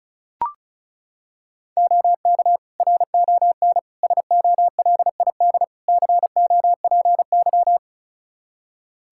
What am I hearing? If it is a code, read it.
Morse code, heard as a keyed tone, about 36 wpm, 700 Hz. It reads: OK RON SOLID COPY
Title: Practice QSO - Exchange 3 - More Info - 35wpm